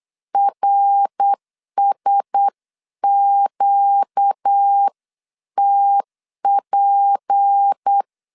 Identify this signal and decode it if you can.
Morse code, heard as a keyed tone, about 8 wpm, 800 Hz. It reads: RSQTP
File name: 8wpm